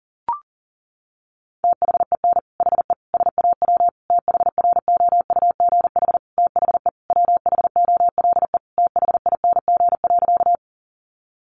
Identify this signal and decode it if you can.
Morse code, heard as a keyed tone, about 40 wpm, 700 Hz. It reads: THEN HE SAW THROUGH THE WHOLE THING.
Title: Sentences from Top 400 Words 40wpm